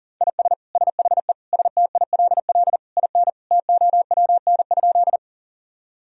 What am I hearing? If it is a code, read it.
Morse code, heard as a keyed tone, about 40 wpm, 700 Hz. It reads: IS SHE STILL IN TOWN?
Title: Sentences from Top 1000 Words - Review Part 2 - 40wpm